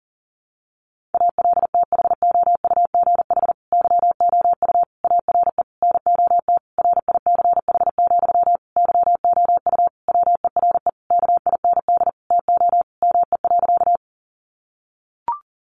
Morse code, heard as a keyed tone, about 40 wpm, 700 Hz. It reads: ALTHOUGH YOU ARE NOT RICH, YOU WERE KIND TO ME.
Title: Sentences from Top 1100 Words 40wpm